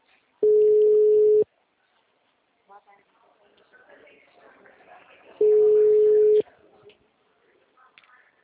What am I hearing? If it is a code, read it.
Ringback tone (Europe, 425 Hz), with speech.